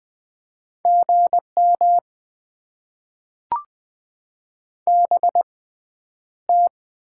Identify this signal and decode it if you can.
Morse code, heard as a keyed tone, about 21 wpm, 700 Hz. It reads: G M B T
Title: Top 20 QSO Elements - 6x Character Spacing - 20wpm